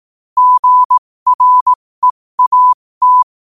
Morse code, heard as a keyed tone, about 18 wpm, spaced as for 15 wpm, 1000 Hz. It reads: GREAT
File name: qotdfiles / 15wpm